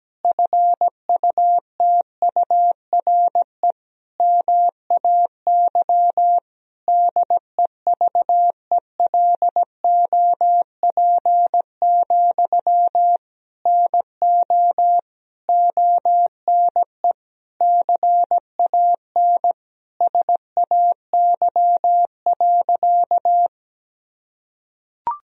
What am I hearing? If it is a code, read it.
Morse code, heard as a keyed tone, about 17 wpm, 700 Hz. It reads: FUTURE MAY DEVELOP, NO ONE CAN SAY.
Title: Sentences from Top 1000 Words - Review Part 2 - 17wpm